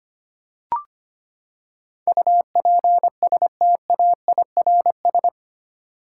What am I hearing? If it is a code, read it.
Morse code, heard as a keyed tone, about 25 wpm, 700 Hz. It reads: UPSTAIRS
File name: Eight Letter Words - 25wpm